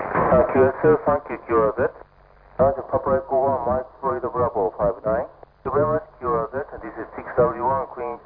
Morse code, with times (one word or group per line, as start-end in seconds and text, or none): none